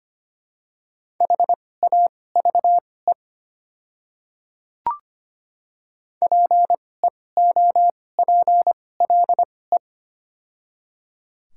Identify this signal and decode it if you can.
Morse code, heard as a keyed tone, about 25 wpm, spaced as for 18 wpm, 700 Hz. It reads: HAVE PEOPLE
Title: 2x Character Spacing - Top 100 Words - 25wpm